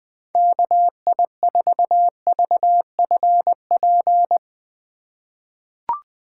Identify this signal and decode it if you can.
Morse code, heard as a keyed tone, about 20 wpm, 700 Hz. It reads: KI4VFP
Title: USA Call Signs - 2x Character Spacing - 20wpm